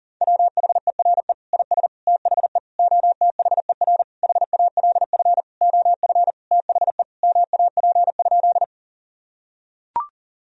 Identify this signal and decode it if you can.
Morse code, heard as a keyed tone, about 40 wpm, 700 Hz. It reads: WHERE IS THE OTHER HALF OF THE MAP?
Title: Sentences from Top 1000 Words - Review Part 2 - 40wpm